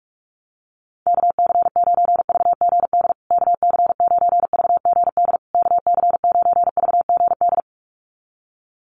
Morse code, heard as a keyed tone, about 44 wpm, 700 Hz. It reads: KC9VGD KC9VGD KC9VGD
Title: USA Call Signs Repeated 3x 45wpm